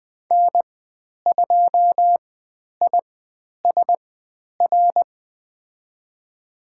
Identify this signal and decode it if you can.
Morse code, heard as a keyed tone, about 20 wpm, spaced as for 10 wpm, 700 Hz. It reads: N2ISR
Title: 102 Lesson 17g - Letter u - Call Signs 20wpm